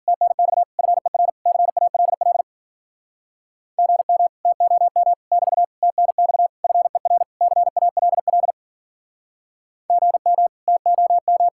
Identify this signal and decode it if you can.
Morse code, heard as a keyed tone, about 52 wpm, 700 Hz. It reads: TNX FER CALL GM TOM <BT> TNX FER CALL GM TOM